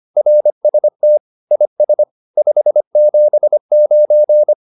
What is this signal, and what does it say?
Morse code, heard as a keyed tone, about 25 wpm, 600 Hz. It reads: RST IS 579
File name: QSO101 25